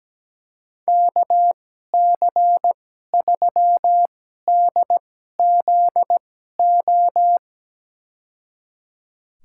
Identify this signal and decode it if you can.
Morse code, heard as a keyed tone, about 17 wpm, spaced as for 12 wpm, 700 Hz. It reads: KC3DZO
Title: USA Call Signs - 2x Character Spacing - 17wpm